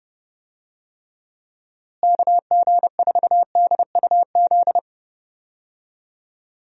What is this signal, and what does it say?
Morse code, heard as a keyed tone, about 30 wpm, 700 Hz. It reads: KG4DUZ
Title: USA Call Signs - 6 Characters - Repeated 3x - 30wpm